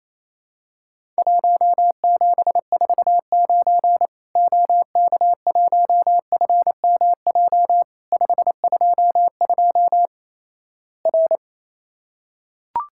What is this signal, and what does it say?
Morse code, heard as a keyed tone, about 28 wpm, 700 Hz. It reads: 1749 OK1FMJ 522 R
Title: Worked All Europe DX Contest - QTC Exchange - 28wpm